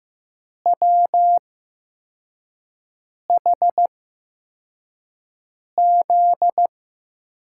Morse code, heard as a keyed tone, about 15 wpm, spaced as for 4 wpm, 700 Hz. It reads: WHZ
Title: USA Call Signs - 8x Character Spacing - 15wpm